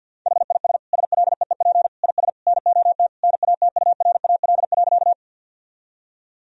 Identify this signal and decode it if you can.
Morse code, heard as a keyed tone, about 50 wpm, 700 Hz. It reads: HIS SLEEP IS NOT NATURAL.
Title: Sentences from Top 1000 Words - Review Part 4 - 50wpm